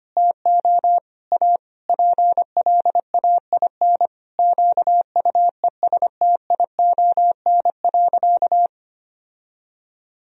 Morse code, heard as a keyed tone, about 25 wpm, 700 Hz. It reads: TO A PLAIN QUESTION.
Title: Sentences from Top 1000 Words - Review Part 2 - 25wpm